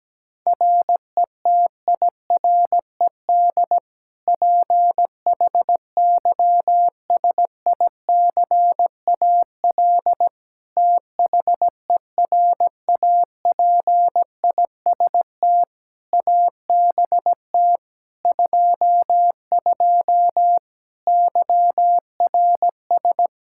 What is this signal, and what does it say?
Morse code, heard as a keyed tone, about 17 wpm, 700 Hz. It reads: RETIRED PHYSICAL THERAPIST ABT 22 YRS
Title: Practice QSO - Exchange 3 - More Info - 17wpm